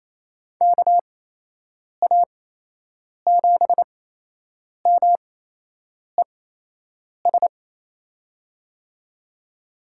Morse code, heard as a keyed tone, about 28 wpm, spaced as for 8 wpm, 700 Hz. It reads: KA7MES
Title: USA Call Signs - 8x Character Spacing - 28wpm